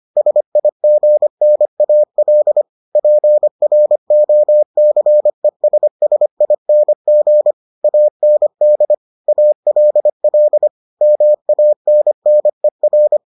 Morse code, heard as a keyed tone, about 25 wpm, 600 Hz. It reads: SIGNAL PROCESSING AND ALL MANNER